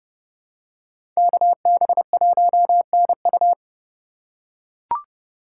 Morse code, heard as a keyed tone, about 30 wpm, 700 Hz. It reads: KB1NU
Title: USA Call Signs - 5 Characters - 3x Character Spacing - 30wpm